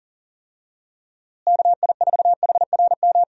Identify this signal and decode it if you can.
Morse code, heard as a keyed tone, about 40 wpm, 700 Hz. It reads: KI4HRM
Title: USA Call Signs - 6 Characters - Repeated 3x - 40wpm